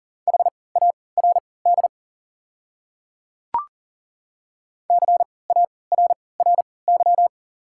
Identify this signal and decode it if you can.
Morse code, heard as a keyed tone, about 40 wpm, spaced as for 23 wpm, 700 Hz. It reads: HARD CARRY
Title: 3x Character Spacing - Top 300 Words - 40wpm